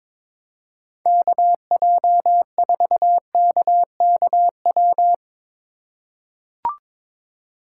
Morse code, heard as a keyed tone, about 22 wpm, 700 Hz. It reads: KJ4KKW
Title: USA Call Signs - 6 Characters - 2x Character Spacing - 22wpm